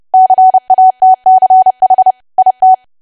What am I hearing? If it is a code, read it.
Morse code, heard as a keyed tone, about 30 wpm, 750 Hz. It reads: CATCH IT
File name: cwa / W1AW-Level3 / 30WPM